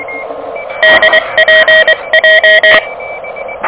Morse code, heard as a keyed tone, about 22 wpm, 1900 Hz, with noise. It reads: DPJ
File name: OK QRP 2023 / OK QRP 2023